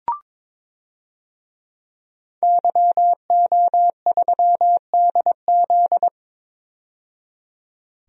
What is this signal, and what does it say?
Morse code, heard as a keyed tone, about 22 wpm, 700 Hz. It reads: YO3DZ
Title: DXCC-100 Call Signs - 5 Characters - 22wpm